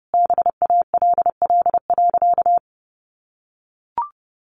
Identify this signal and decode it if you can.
Morse code, heard as a keyed tone, about 30 wpm, 700 Hz. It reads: BALL.
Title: Additional Easy Phrases 30wpm